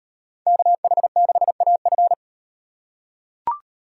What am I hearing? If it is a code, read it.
Morse code, heard as a keyed tone, about 38 wpm, 700 Hz. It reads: KH6AF